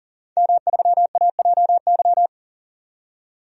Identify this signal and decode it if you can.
Morse code, heard as a keyed tone, about 40 wpm, 700 Hz. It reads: M3AJY
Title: DXCC-100 Call Signs - 5 Characters - 40wpm